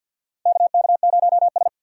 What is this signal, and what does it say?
Morse code, heard as a keyed tone, about 50 wpm, 690 Hz. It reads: KK0S
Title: USA Call Signs - 4 Characters - 6x Character Spacing - 50wpm